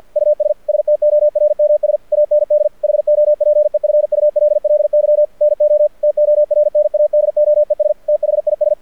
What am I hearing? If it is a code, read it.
Morse code, heard as a keyed tone, about 53 wpm, 600 Hz. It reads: US STORMS AND HOPEFULLY NO TORNADOES THIS
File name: archived / CRSnet